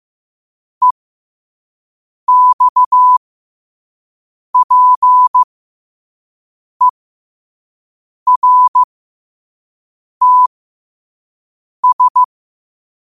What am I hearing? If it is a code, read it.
Morse code, heard as a keyed tone, about 15 wpm, spaced as for 5 wpm, 1000 Hz. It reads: EXPERTS
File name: qotdfiles / 5wpm